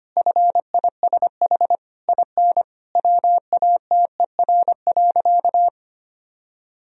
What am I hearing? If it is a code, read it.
Morse code, heard as a keyed tone, about 25 wpm, 700 Hz. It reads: FISH IN WATER.
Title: Sentences from Top 1000 Words - Review Part 1 - 25wpm